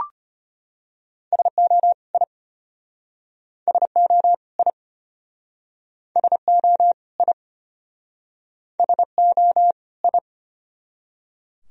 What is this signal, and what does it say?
Morse code, heard as a keyed tone, about 33 wpm, 700 Hz. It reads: SO I SO I SO I SO I